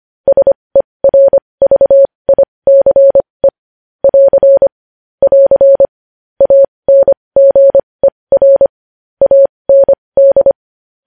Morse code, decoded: SERVICE <AR> <AR> ANGER AND

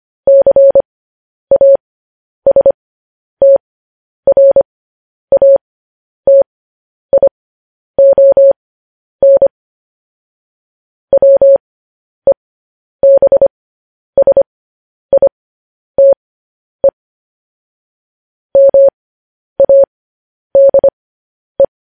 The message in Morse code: CASTRATION WEBSITE MADE

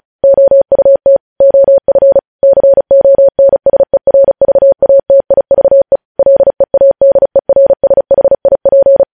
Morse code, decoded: OUT OF CONSERVATIVE LEADERSHIP

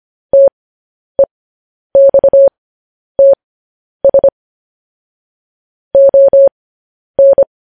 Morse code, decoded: TEXTS ON

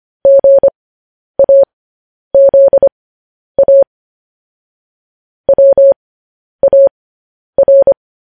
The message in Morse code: GAZA WAR

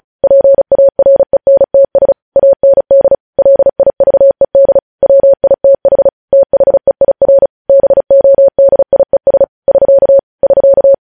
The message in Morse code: PARENTS AND LIVED WITH THEIR BODIES <SK> <SK>

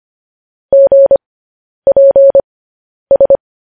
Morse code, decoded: GPS

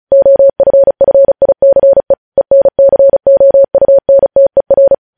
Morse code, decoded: OFFICE ENCOUNTER

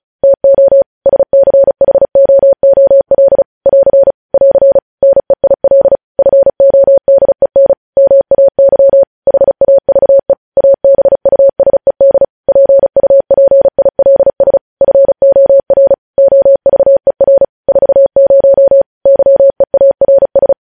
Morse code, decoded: TO SCHOOL <AR> <AR> NEIL FODEN MAY HAVE ABUSED PUPILS FOR OVER 40 YEARS

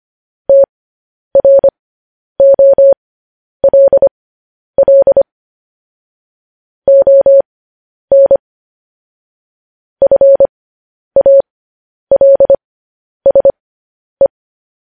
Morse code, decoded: TROLL ON FALSE

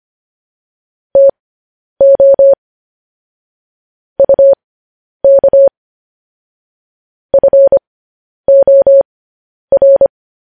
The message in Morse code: TO UK FOR

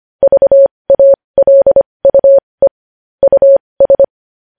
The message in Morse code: VALUE US

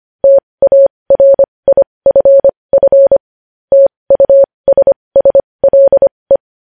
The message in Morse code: TARIFF TUSSLE

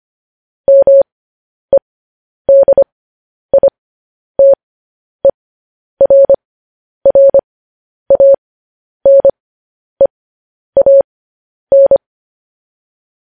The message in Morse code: MEDITERRANEAN

